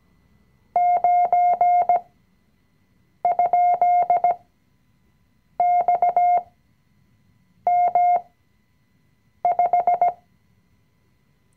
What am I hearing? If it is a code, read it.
Morse code, heard as a keyed tone, about 17 wpm, spaced as for 6 wpm, 710 Hz. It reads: 9?XM5
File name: CW-Online-Kurs / Anfänger / Lektion 020